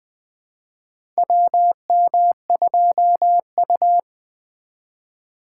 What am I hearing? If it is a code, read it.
Morse code, heard as a keyed tone, about 20 wpm, 700 Hz. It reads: WM2U
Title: USA Call Signs - 4 Characters - 4x Character Spacing - 20wpm